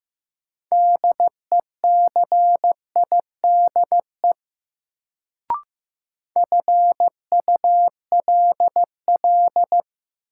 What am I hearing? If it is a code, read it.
Morse code, heard as a keyed tone, about 15 wpm, 700 Hz. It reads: DECIDE FULL
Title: Sentences from Top 500 Words - New Word Drill - 15wpm